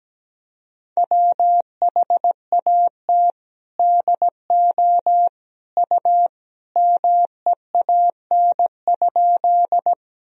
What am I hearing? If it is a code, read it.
Morse code, heard as a keyed tone, about 17 wpm, 700 Hz. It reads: WHAT DO U MEAN?
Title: Sentences from Top 200 Words and Abbreviations 17wpm